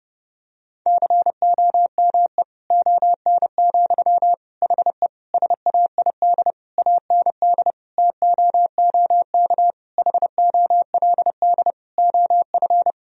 Morse code, heard as a keyed tone, about 32 wpm, 700 Hz. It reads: COME ON, HE SAID AND TOOK HOLD OF
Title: Sentences from Top 1000 Words - Review Part 2 - 30wpm